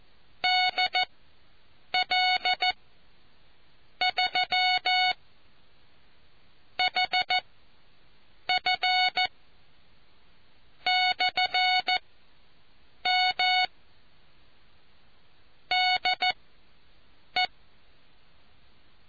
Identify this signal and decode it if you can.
Morse code, heard as a keyed tone, about 14 wpm, spaced as for 6 wpm, 740 Hz. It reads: DL3HF/M DE